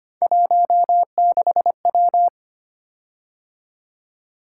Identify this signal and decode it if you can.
Morse code, heard as a keyed tone, about 25 wpm, 700 Hz. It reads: 16W
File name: Three Letter-Number Combinations - 25wpm